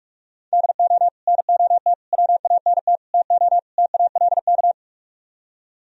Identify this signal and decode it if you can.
Morse code, heard as a keyed tone, about 48 wpm, 700 Hz. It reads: DO NOT WANT TO TALK